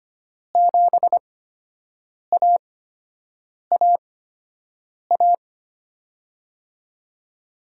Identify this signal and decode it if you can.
Morse code, heard as a keyed tone, about 25 wpm, spaced as for 7 wpm, 700 Hz. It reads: 7AAA
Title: USA Call Signs - 8x Character Spacing - 25wpm